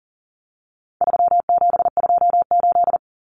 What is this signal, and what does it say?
Morse code, heard as a keyed tone, about 40 wpm, 700 Hz. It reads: 3728
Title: Common Cut Numbers - 4-Digit 40wpm